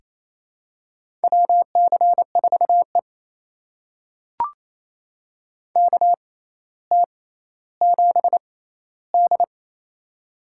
Morse code, heard as a keyed tone, about 27 wpm, spaced as for 14 wpm, 700 Hz. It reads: WC4E KT7D
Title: USA Call Signs - 4 Characters - 6x Character Spacing - 28wpm